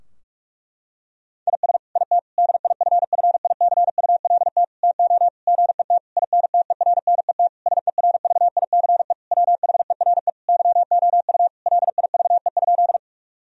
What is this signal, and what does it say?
Morse code, heard as a keyed tone, about 46 wpm, 700 Hz. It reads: IS IT DIFFICULT TO GET INTERNET SERVICE WHERE YOU LIVE?
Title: Sentences from Top 1000 Words - Review Part 4 - 45wpm